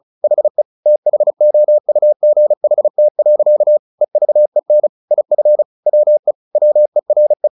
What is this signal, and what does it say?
Morse code, heard as a keyed tone, about 34 wpm, 600 Hz. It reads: HE THOUGHT. EVEN IF WE WERE
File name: the old man and the sea 2 35wpm